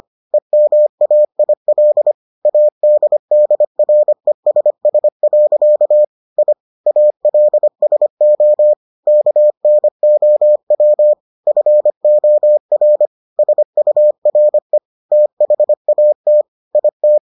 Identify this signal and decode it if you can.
Morse code, heard as a keyed tone, about 25 wpm, 600 Hz. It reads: EMAIL ADDRESS. I ALSO KNOW FOR SURE THAT IT